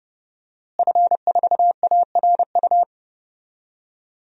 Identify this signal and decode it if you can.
Morse code, heard as a keyed tone, about 32 wpm, 700 Hz. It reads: F4ARU
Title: DXCC-100 Call Signs - 5 Characters - Repeated 3x - 30wpm